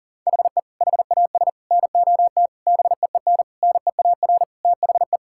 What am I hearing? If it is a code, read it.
Morse code, heard as a keyed tone, about 37 wpm, 700 Hz. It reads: HE HAS NOT BEEN NEAR THE
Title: Sentences from Top 1000 Words - Review Part 6 - 40wpm